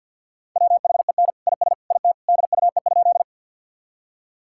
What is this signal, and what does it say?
Morse code, heard as a keyed tone, about 50 wpm, 700 Hz. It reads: WHEN IS IT DUE?
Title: Sentences from Top 1000 Words - Review Part 6 - 50wpm